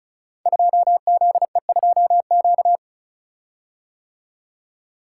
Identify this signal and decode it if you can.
Morse code, heard as a keyed tone, about 34 wpm, 700 Hz. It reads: 2ZE2Q
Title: Code Groups - 5 Letter:Numbers - 35wpm